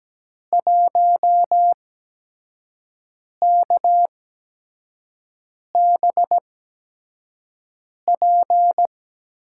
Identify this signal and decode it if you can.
Morse code, heard as a keyed tone, about 17 wpm, spaced as for 5 wpm, 700 Hz. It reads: 1KBP
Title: USA Call Signs - 5 Characters - 8x Character Spacing - 17wpm